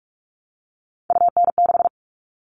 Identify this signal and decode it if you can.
Morse code, heard as a keyed tone, about 44 wpm, 700 Hz. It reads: UN6
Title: All Cut Numbers - 3-Digit 45wpm